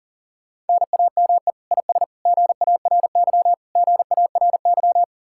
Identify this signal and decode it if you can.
Morse code, heard as a keyed tone, about 52 wpm, 700 Hz. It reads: NAME IS GARY GARY